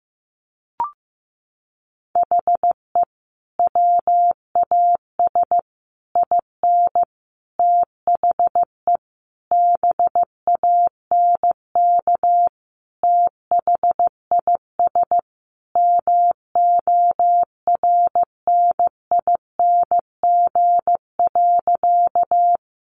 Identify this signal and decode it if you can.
Morse code, heard as a keyed tone, about 15 wpm, 700 Hz. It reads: HE WAS IN THE BANK THIS MORNING.
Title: Sentences from Top 900 Words 15wpm